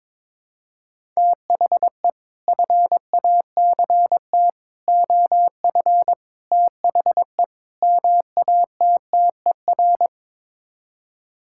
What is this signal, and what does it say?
Morse code, heard as a keyed tone, about 22 wpm, 700 Hz. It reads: THE FACT OF THE MATTER
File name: practice / 22wpm / Build A Phrase - 5-Word Phrases - 22wpm